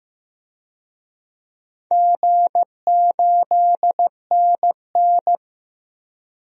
Morse code, heard as a keyed tone, about 15 wpm, 700 Hz. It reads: G8NN
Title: DXCC-100 Call Signs - 4 Characters - 15wpm